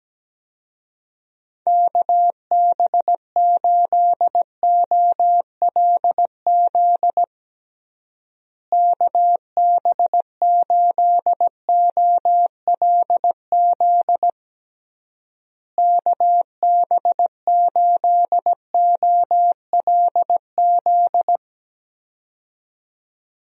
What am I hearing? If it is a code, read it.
Morse code, heard as a keyed tone, about 17 wpm, 700 Hz. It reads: KB8OLZ KB8OLZ KB8OLZ
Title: USA Call Signs - 6 Characters - Repeated 3x - 17wpm